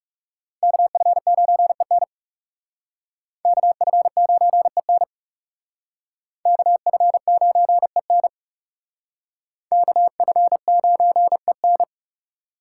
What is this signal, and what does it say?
Morse code, heard as a keyed tone, about 37 wpm, 700 Hz. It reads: KF9EN KF9EN KF9EN KF9EN
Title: Speed Racing - USA Call Signs 30wpm